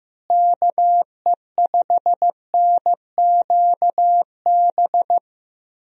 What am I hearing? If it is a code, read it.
Morse code, heard as a keyed tone, about 15 wpm, 700 Hz. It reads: KE5NQB
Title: USA Call Signs - 6 Characters - Repeated 2x - 15wpm